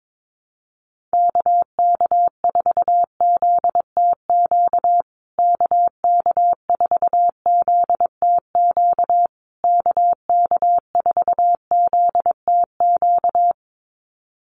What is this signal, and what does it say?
Morse code, heard as a keyed tone, about 22 wpm, 690 Hz. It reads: KK4ZTQ KK4ZTQ KK4ZTQ
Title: USA Call Signs Repeated 3x 22wpm